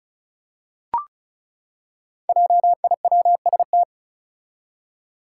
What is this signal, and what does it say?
Morse code, heard as a keyed tone, about 36 wpm, 700 Hz. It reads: JIWST
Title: Five Challenging Letter-Number Combinations 35wpm